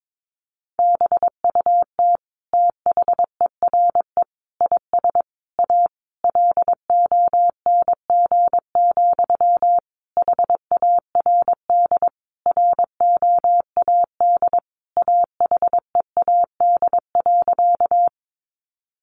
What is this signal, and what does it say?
Morse code, heard as a keyed tone, about 22 wpm, 700 Hz. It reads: BUT THERE IS A LONG, HARD ROAD AHEAD.
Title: Sentences from Top 1100 Words 22wpm